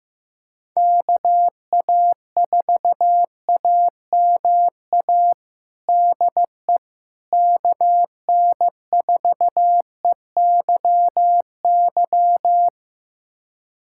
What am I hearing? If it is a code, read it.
Morse code, heard as a keyed tone, about 15 wpm, 700 Hz. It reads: KA4AMA DE KN4EYY
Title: USA Callsign DE Callsign 15wpm